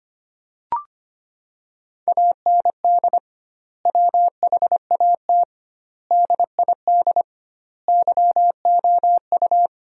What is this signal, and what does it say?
Morse code, heard as a keyed tone, about 25 wpm, 700 Hz. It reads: AND WHAT DID YOU
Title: 2x Word-Spacing - Sentences from Top 300 Words - 25wpm